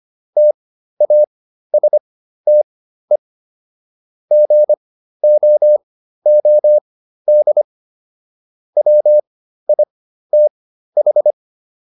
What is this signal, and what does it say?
Morse code, heard as a keyed tone, about 25 wpm, spaced as for 13 wpm, 600 Hz. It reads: TASTE GOOD WITH
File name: SS109 13